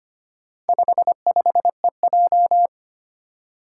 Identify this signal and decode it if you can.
Morse code, heard as a keyed tone, about 25 wpm, 700 Hz. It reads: 55EJ
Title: Four Challenging Letter-Number Combinations 25wpm